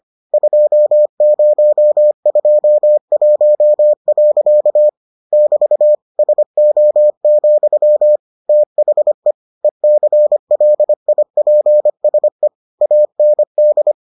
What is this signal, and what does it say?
Morse code, heard as a keyed tone, about 25 wpm, 600 Hz. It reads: 2021. <BT> SO, THE ECLIPSE AND